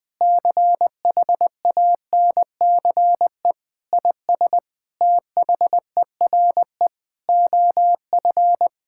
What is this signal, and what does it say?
Morse code, heard as a keyed tone, about 20 wpm, 700 Hz. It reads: CHANCE IS THERE OF
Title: Sentences from Top 1000 Words - Review Part 6 - 20wpm